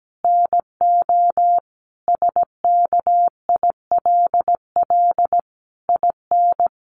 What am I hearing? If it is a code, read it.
Morse code, heard as a keyed tone, about 17 wpm, 700 Hz. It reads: NO SKILL IN
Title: Sentences from Top 1200 Words 17wpm